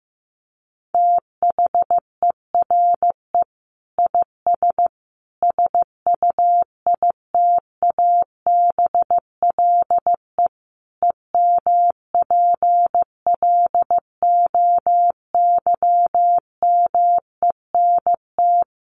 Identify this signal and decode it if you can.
Morse code, heard as a keyed tone, about 15 wpm, 700 Hz. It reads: THERE IS SUITABLE EMPLOYMENT
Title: Sentences from Top 1500 Words 15wpm